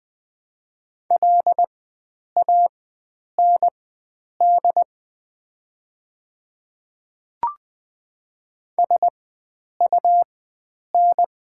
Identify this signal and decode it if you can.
Morse code, heard as a keyed tone, about 20 wpm, spaced as for 9 wpm, 700 Hz. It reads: LAND SUN
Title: 4x Character Spacing - Top 300 Words - 20wpm